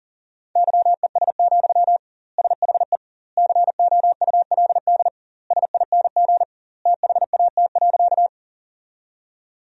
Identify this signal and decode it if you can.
Morse code, heard as a keyed tone, about 37 wpm, 700 Hz. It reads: YES, SHE COULD SING THAT.
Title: 2x Word-Spacing - Sentences from Top 500 Words - 40wpm